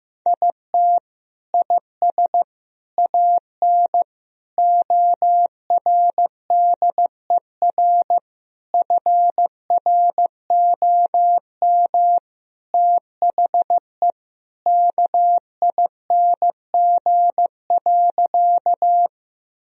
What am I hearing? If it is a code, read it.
Morse code, heard as a keyed tone, about 15 wpm, 700 Hz. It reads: IT IS AN ORDER FROM THE KING.
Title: Sentences from Top 1000 Words - Review Part 2 - 15wpm